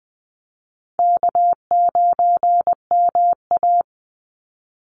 Morse code, heard as a keyed tone, about 20 wpm, 700 Hz. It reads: K9MA
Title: Call Signs Repeated 3x - 20wpm